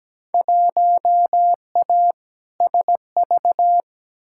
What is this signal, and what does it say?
Morse code, heard as a keyed tone, about 17 wpm, 700 Hz. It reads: 1A SV
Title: Field Day Exchanges Encore - 17wpm